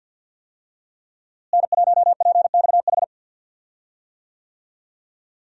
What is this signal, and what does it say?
Morse code, heard as a keyed tone, about 47 wpm, 700 Hz. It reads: N1PXH
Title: USA Call Signs - 5 Characters - Repeated 3x - 50wpm